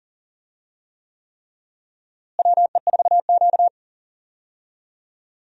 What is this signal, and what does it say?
Morse code, heard as a keyed tone, about 40 wpm, 700 Hz. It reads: WE4Q
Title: USA Call Signs - 4 Characters - Repeated 3x - 40wpm